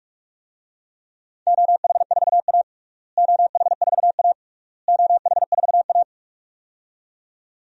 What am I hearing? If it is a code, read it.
Morse code, heard as a keyed tone, about 46 wpm, 700 Hz. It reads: OH4A OH4A OH4A
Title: DXCC-100 Call Signs - 4 Characters - Repeated 3x - 45wpm